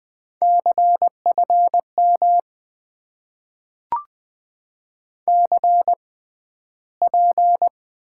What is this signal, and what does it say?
Morse code, heard as a keyed tone, about 20 wpm, spaced as for 12 wpm, 700 Hz. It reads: CFM CP
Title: Top 90 QSO Elements - 6x Character Spacing - 20wpm